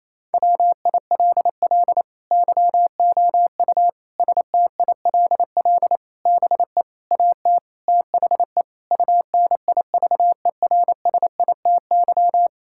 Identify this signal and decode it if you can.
Morse code, heard as a keyed tone, about 28 wpm, 700 Hz. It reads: WILL YOU STILL BE AT THE UNIVERSITY
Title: Sentences from Top 1000 Words - Review Part 5 - 28wpm